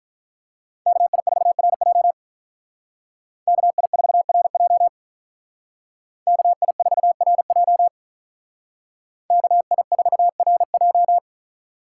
Morse code, heard as a keyed tone, about 45 wpm, 700 Hz. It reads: KI4RJ KI4RJ KI4RJ KI4RJ